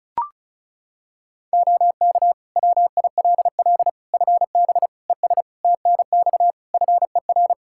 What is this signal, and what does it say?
Morse code, heard as a keyed tone, about 34 wpm, 700 Hz. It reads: OK WILL FB ES TNX FER
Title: Practice QSO - Exchange 2 - Info - 35wpm